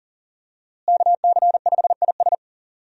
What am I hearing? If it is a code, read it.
Morse code, heard as a keyed tone, about 39 wpm, 700 Hz. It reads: KC5IS
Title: USA Call Signs - 5 Characters - Repeated 3x - 40wpm